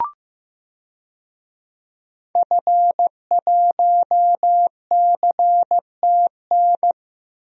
Morse code, heard as a keyed tone, about 15 wpm, 700 Hz. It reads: F1CTN
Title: DXCC-100 Call Signs - 5 Characters - 15wpm